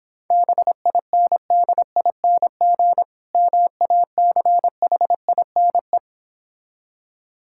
Morse code, heard as a keyed tone, about 26 wpm, 700 Hz. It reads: BINDING MACHINE